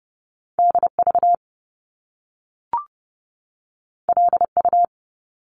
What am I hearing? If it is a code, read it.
Morse code, heard as a keyed tone, about 30 wpm, 700 Hz. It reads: D V L U
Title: Top Two Letter Combinations in Words 30wpm